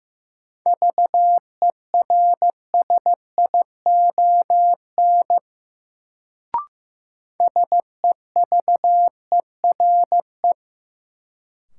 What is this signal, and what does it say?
Morse code, heard as a keyed tone, about 15 wpm, 700 Hz. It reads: VERSION SEVERE
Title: Sentences from Top 1400 Words - New Word Drill Alt - 15wpm